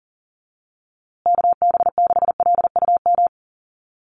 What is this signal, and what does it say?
Morse code, heard as a keyed tone, about 40 wpm, 700 Hz. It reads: KB6LUM
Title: USA Call Signs Repeated 2x 40wpm